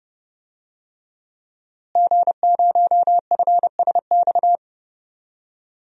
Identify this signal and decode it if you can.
Morse code, heard as a keyed tone, about 30 wpm, 700 Hz. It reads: G0FSX
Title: DXCC-100 Call Signs - 5 Characters - Repeated 3x - 30wpm